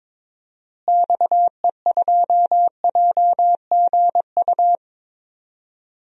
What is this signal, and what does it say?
Morse code, heard as a keyed tone, about 22 wpm, 700 Hz. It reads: XE2JGU
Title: DXCC-100 Call Signs - 6 Characters - Repeated 2x - 22wpm